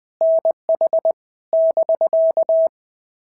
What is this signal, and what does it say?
Morse code, heard as a keyed tone, about 19 wpm, 650 Hz. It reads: NH <BK>
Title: POTA QSO Practice - 20wpm